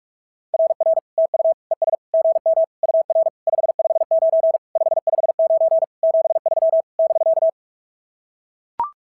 Morse code, heard as a keyed tone, about 44 wpm, 690 Hz. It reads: RR TU ES GM UR 559 559 73 <BK>
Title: SOTA QSO Practice 45wpm